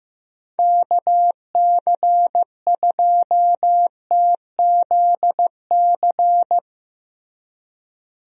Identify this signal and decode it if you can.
Morse code, heard as a keyed tone, about 15 wpm, 700 Hz. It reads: KC2TZC
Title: USA Call Signs - 6 Characters - Repeated 2x - 15wpm